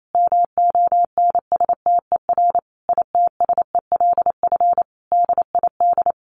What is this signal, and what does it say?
Morse code, heard as a keyed tone, about 28 wpm, 700 Hz. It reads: MONSTER ITSELF DID
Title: A Sound Of Thunder 28wpm